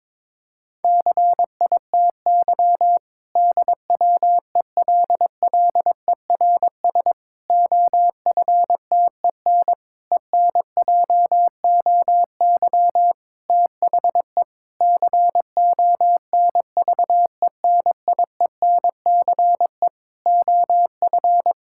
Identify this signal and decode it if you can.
Morse code, heard as a keyed tone, about 22 wpm, 700 Hz. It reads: CITY DWELLERS OFTEN ENJOY THE CONVENIENCE OF
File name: Difficult Sentences with Hints - 22wpm